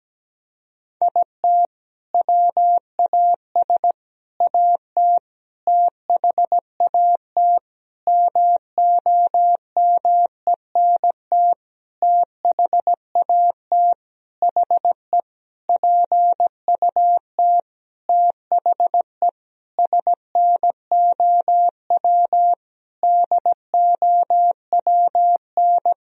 Morse code, heard as a keyed tone, about 17 wpm, 700 Hz. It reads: IT WAS AT THAT MOMENT THAT HE PUT THE SNOW DOWN
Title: Sentences from Top 1000 Words - Review Part 3 - 17wpm